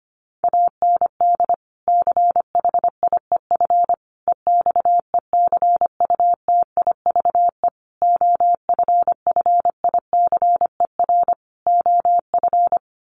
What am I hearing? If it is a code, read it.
Morse code, heard as a keyed tone, about 25 wpm, 700 Hz. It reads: AND CHIEF EXECUTIVE OFFICER OF
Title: Sets of 5 Words Encore2 25wpm